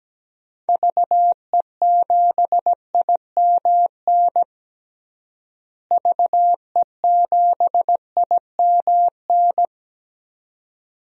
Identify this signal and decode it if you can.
Morse code, heard as a keyed tone, about 17 wpm, 700 Hz. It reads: VE7IMN VE7IMN
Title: DXCC-100 Call Signs - 6 Characters - Repeated 2x - 17wpm